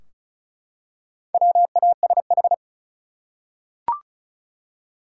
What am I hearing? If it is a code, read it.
Morse code, heard as a keyed tone, about 36 wpm, 700 Hz. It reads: WASH
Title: Four Letter Words - 35wpm